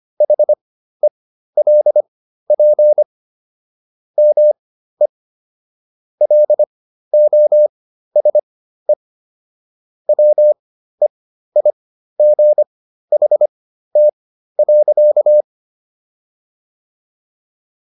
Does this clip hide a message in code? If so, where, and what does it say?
Yes: Morse code from 0.2 to 15.4 s, heard as a keyed tone, about 25 wpm, spaced as for 13 wpm, 600 Hz. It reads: HELP ME LOSE WEIGHT.